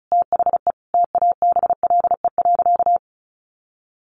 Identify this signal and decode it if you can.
Morse code, heard as a keyed tone, about 34 wpm, 700 Hz. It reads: THE TABLE.
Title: Sentences from Top 1100 Words 35wpm